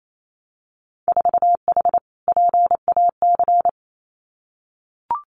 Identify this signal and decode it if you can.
Morse code, heard as a keyed tone, about 28 wpm, 690 Hz. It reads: 4H PAC
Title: Winter Field Day Exchanges Encore 28wpm